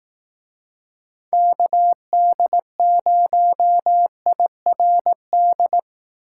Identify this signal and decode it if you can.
Morse code, heard as a keyed tone, about 18 wpm, 700 Hz. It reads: KD0IRD
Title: Speed Racing - USA 2x3 Call Signs 15wpm